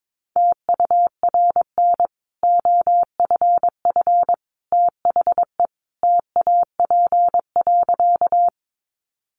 Morse code, heard as a keyed tone, about 22 wpm, 700 Hz. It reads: TURN OFF THE TAP.
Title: Sentences from Top 1400 Words 22wpm